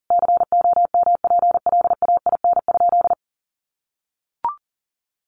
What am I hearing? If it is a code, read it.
Morse code, heard as a keyed tone, about 40 wpm, 700 Hz. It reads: COMPLAIN?
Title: Sentences from Top 1300 Words 40wpm